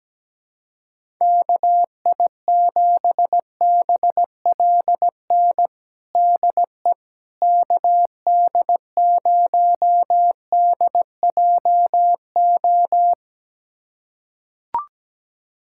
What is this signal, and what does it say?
Morse code, heard as a keyed tone, about 17 wpm, 700 Hz. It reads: KI7BLN DE KD0DJO
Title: USA Callsign DE Callsign 17wpm